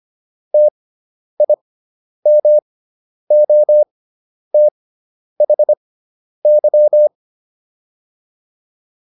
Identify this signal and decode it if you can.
Morse code, heard as a keyed tone, about 25 wpm, spaced as for 10 wpm, 600 Hz. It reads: TIMOTHY